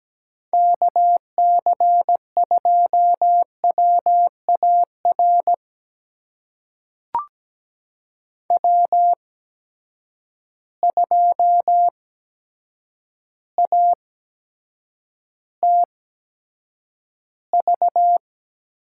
Morse code, heard as a keyed tone, about 17 wpm, spaced as for 8 wpm, 700 Hz. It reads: KC2WAR W2ATV
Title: USA Call Signs - 8x Character Spacing - 17wpm